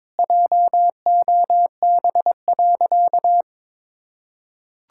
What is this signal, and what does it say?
Morse code, heard as a keyed tone, about 22 wpm, 700 Hz. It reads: JOB.
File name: practice / 22wpm / Sentences with 3-Word Phrases - 22wpm